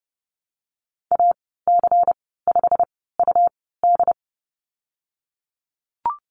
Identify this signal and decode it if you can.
Morse code, heard as a keyed tone, about 30 wpm, spaced as for 17 wpm, 700 Hz. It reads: AC5UD
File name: Lesson 19g - Letter w - Call Signs 30wpm